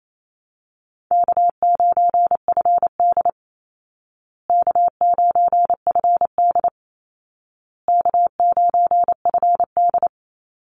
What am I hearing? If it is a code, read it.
Morse code, heard as a keyed tone, about 28 wpm, 700 Hz. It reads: K9FD K9FD K9FD
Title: Call Signs Repeated 2x - 28wpm